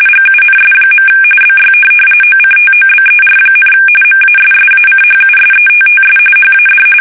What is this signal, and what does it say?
RTTY, about 45.45 baud. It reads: GUNS. BY OCTOBER, MASSACHUSETTS HAD
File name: fsk850 / USS-Massachusetts